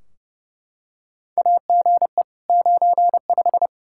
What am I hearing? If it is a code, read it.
Morse code, heard as a keyed tone, about 28 wpm, 700 Hz. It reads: AGE 95
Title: K1USN SST QSO Practice 30wpm